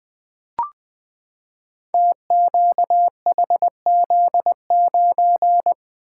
Morse code, heard as a keyed tone, about 20 wpm, 700 Hz. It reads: TQHZ9
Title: Code Groups - 5 Letter:Numbers - 20wpm